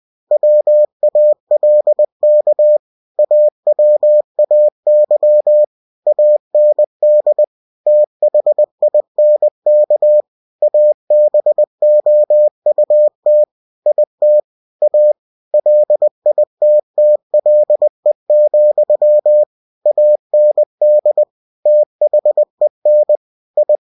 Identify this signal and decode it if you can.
Morse code, heard as a keyed tone, about 20 wpm, 600 Hz. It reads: WALK AWAY AND THINK ABOUT IT A LITTLE, AND THEN I
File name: cw-books / winnie-the-pooh / winnie-20wpm-page-0036